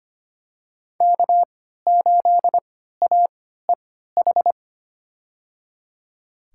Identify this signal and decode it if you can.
Morse code, heard as a keyed tone, about 25 wpm, spaced as for 14 wpm, 700 Hz. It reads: K8AEH
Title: USA Call Signs - 5 Characters - 3x Character Spacing - 25wpm